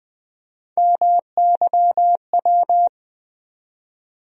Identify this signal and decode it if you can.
Morse code, heard as a keyed tone, about 20 wpm, 700 Hz. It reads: MYW
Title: USA Call Sign Postfixes - 3 Letters 20wpm